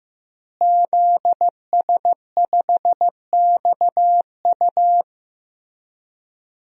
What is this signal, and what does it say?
Morse code, heard as a keyed tone, about 16 wpm, 700 Hz. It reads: ZS5XU
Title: DXCC-100 Call Signs - 5 Characters - Repeated 3x - 15wpm